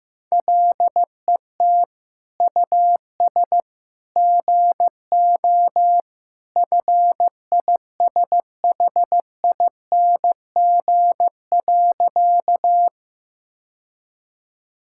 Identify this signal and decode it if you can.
Morse code, heard as a keyed tone, about 15 wpm, 700 Hz. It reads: LET US GO FISHING.
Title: Sentences from Top 1000 Words - Review Part 5 - 15wpm